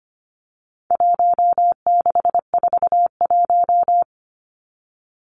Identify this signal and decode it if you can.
Morse code, heard as a keyed tone, about 25 wpm, 700 Hz. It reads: 1641
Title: Numbers - Contest Style - 4-Digit 25wpm